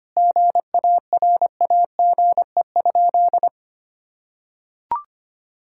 Morse code, heard as a keyed tone, about 25 wpm, 700 Hz. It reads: GARAGE?
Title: Sentences from Top 1600 Words 25wpm